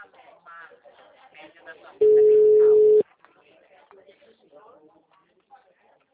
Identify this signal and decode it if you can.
Ringback tone (Europe, 425 Hz), with speech.
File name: out- -2009- - - .21906